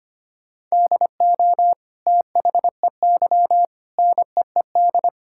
Morse code, heard as a keyed tone, about 25 wpm, 700 Hz. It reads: DO THEY NEED
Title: Sentences from Top 1000 Words - Review Part 5 - 25wpm